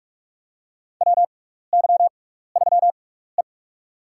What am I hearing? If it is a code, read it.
Morse code, heard as a keyed tone, about 44 wpm, spaced as for 15 wpm, 700 Hz. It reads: WY3E
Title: USA Call Signs - 4 Characters - 6x Character Spacing - 45wpm